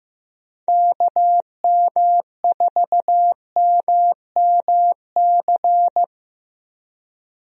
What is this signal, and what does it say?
Morse code, heard as a keyed tone, about 15 wpm, 700 Hz. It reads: KM4MMC
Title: USA Call Signs - 6 Characters - Repeated 2x - 15wpm